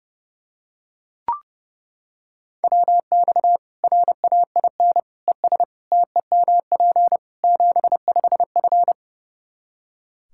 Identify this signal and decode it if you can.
Morse code, heard as a keyed tone, about 32 wpm, 700 Hz. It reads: WX RAIN ES TEMP 75F
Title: Practice QSO - Exchange 2 - Info - 30wpm